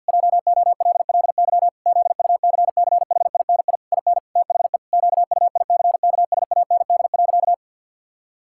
Morse code, heard as a keyed tone, about 50 wpm, 700 Hz. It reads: JOLLY ZUCCHINI IN THE QUICKSAND.
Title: Pangram Challenge v2 - 50wpm